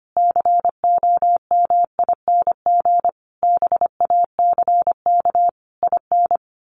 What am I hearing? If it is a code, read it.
Morse code, heard as a keyed tone, about 25 wpm, 700 Hz. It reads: COMING BACK IN
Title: A Sound Of Thunder 25wpm